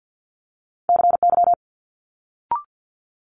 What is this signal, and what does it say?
Morse code, heard as a keyed tone, about 50 wpm, 700 Hz. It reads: CY
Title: Multi-letter Phonemes Quick Fire 50wpm